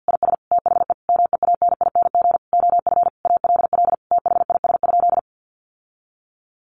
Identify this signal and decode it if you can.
Morse code, heard as a keyed tone, about 49 wpm, 700 Hz. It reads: IS THE MEANING OF ALL THIS?
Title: Sentences from Top 900 Words 50wpm